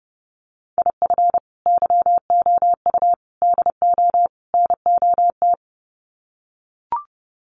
Morse code, heard as a keyed tone, about 29 wpm, 700 Hz. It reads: IF YOU DO NOT
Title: Sets of 4 Words 30wpm